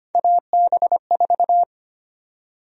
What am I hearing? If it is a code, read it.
Morse code, heard as a keyed tone, about 25 wpm, 700 Hz. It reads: AB4
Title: USA Call Sign Prefixes - Repeated 2x - 25wpm